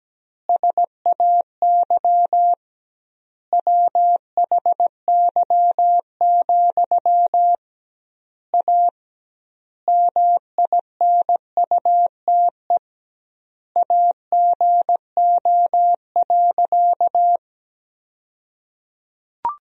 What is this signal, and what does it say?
Morse code, heard as a keyed tone, about 17 wpm, 700 Hz. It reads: SAY WHY, A MINUTE AGO.
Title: 2x Word-Spacing - Sentences from Top 500 Words - 17wpm